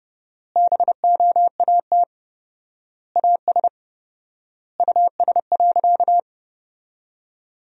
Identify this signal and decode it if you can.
Morse code, heard as a keyed tone, about 30 wpm, 700 Hz. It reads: BOAT AS US.
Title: 4x Word-Spacing - Sentences from Top 500 Words - 30wpm